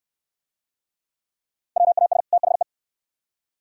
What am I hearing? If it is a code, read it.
Morse code, heard as a keyed tone, about 69 wpm, 700 Hz. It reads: WAS THE